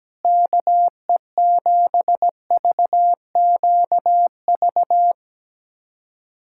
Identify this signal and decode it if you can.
Morse code, heard as a keyed tone, about 17 wpm, 700 Hz. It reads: KE7VQV
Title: USA Callsign DE Callsign - 6 Characters - 17wpm